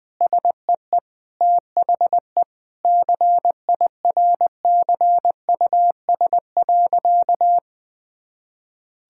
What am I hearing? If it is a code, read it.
Morse code, heard as a keyed tone, about 20 wpm, 700 Hz. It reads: SEE THE CIRCUS.
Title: Difficult Sentences with Hints - 20wpm